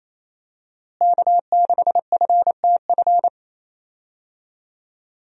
Morse code, heard as a keyed tone, about 28 wpm, 700 Hz. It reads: K6FTF
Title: USA Call Signs - 5 Characters - Repeated 3x - 28wpm